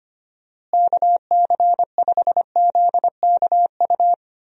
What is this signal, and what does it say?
Morse code, heard as a keyed tone, about 25 wpm, 700 Hz. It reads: KC5ZKU
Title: USA Call Signs - 6 Characters - Repeated 3x - 25wpm